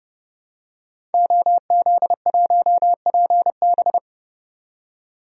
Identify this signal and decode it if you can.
Morse code, heard as a keyed tone, about 30 wpm, 700 Hz. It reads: OZ1PB
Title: DXCC-100 Call Signs - 5 Characters - 30wpm